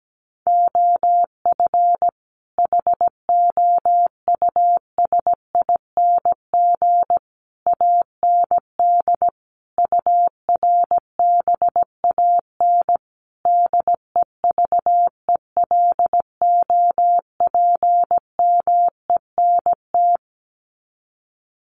Morse code, heard as a keyed tone, about 17 wpm, 700 Hz. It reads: OF HOUSING AND URBAN DEVELOPMENT
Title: Sets of 5 Words Encore2 17wpm